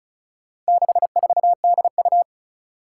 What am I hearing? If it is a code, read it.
Morse code, heard as a keyed tone, about 36 wpm, 700 Hz. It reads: 64DU
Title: Four Challenging Letter-Number Combinations 35wpm